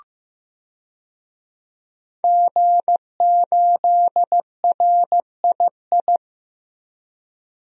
Morse code, heard as a keyed tone, about 15 wpm, 700 Hz. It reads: G8RII
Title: DXCC-100 Call Signs - 5 Characters - Repeated 3x - 15wpm